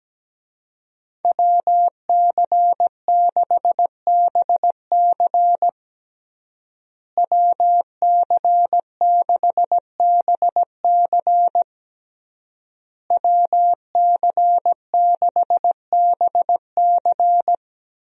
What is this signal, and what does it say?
Morse code, heard as a keyed tone, about 17 wpm, 700 Hz. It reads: WC6BC WC6BC WC6BC
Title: USA Call Signs - 5 Characters - Repeated 3x - 17wpm